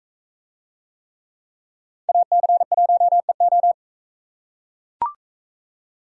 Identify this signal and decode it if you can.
Morse code, heard as a keyed tone, about 44 wpm, 700 Hz. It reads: AC1EO